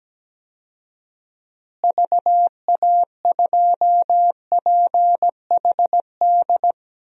Morse code, heard as a keyed tone, about 17 wpm, 700 Hz. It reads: VA2PHD
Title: DXCC-100 Call Signs - 6 Characters - Repeated 2x - 17wpm